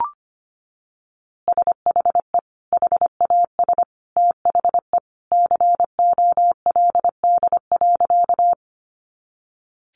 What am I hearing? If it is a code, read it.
Morse code, heard as a keyed tone, about 25 wpm, 700 Hz. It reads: SHE HAS THE COLD.
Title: Easy Phrases 25wpm